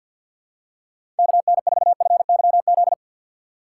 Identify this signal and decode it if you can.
Morse code, heard as a keyed tone, about 52 wpm, 700 Hz. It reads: KN4FYZ
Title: USA Call Signs - 2x Character Spacing - 50wpm